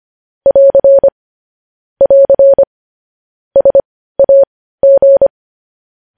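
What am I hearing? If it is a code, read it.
Morse code, heard as a keyed tone, about 25 wpm, spaced as for 15 wpm, 550 Hz. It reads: <AR> <AR> SAG
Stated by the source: Morse practice of news headlines at 15 wpm.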